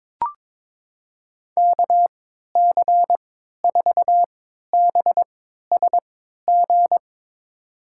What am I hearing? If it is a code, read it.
Morse code, heard as a keyed tone, about 21 wpm, spaced as for 12 wpm, 700 Hz. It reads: KC4BSG
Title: USA Call Signs - 6 Characters - 3x Character Spacing - 22wpm